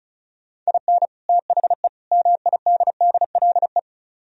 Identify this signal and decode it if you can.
Morse code, heard as a keyed tone, about 46 wpm, 700 Hz. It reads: IN THE MIDDLE